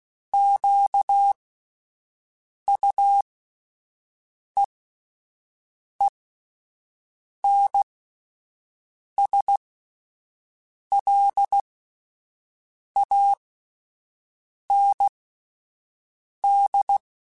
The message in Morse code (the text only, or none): QUEENSLAND